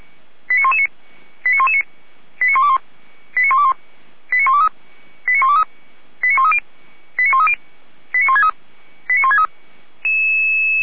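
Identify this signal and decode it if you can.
Five-tone selective call (ZVEI), heard as quick runs of five tones, tones of about 80 ms. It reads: 89119 89119 89121 89121 89123 89123 89130 89130 89163 89163
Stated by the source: ZVEI five-tone selective call received by SDR, Feuerwehr Orlamünde.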